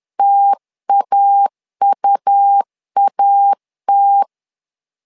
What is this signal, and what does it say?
Morse code, heard as a keyed tone, about 10 wpm, 800 Hz. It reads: TAUAT
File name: LEZIONI / 2 UHDN / 2 U 10wpm 1622